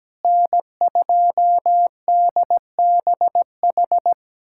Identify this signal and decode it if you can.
Morse code, heard as a keyed tone, about 17 wpm, 700 Hz. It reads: N2DBH
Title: USA Callsign DE Callsign - 5 Characters - 17wpm